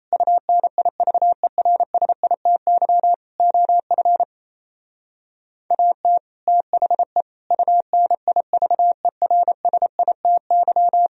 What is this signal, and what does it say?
Morse code, heard as a keyed tone, about 35 wpm, 700 Hz. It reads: UNIVERSITY OF AT THE UNIVERSITY